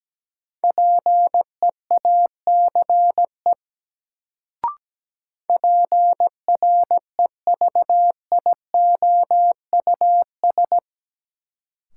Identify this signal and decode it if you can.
Morse code, heard as a keyed tone, about 17 wpm, 700 Hz. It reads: PEACE PREVIOUS
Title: Sentences from Top 1500 Words - New Word Drill - 17wpm